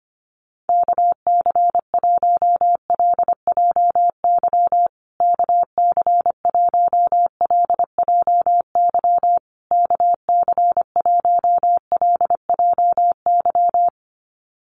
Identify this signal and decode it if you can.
Morse code, heard as a keyed tone, about 25 wpm, 700 Hz. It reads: KC1LJY KC1LJY KC1LJY
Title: USA Call Signs Repeated 3x 25wpm